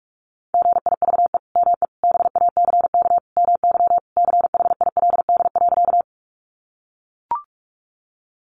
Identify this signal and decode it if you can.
Morse code, heard as a keyed tone, about 44 wpm, 700 Hz. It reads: GIVE ME BACK MY CHILD.
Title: Sentences from Top 800 Words 45wpm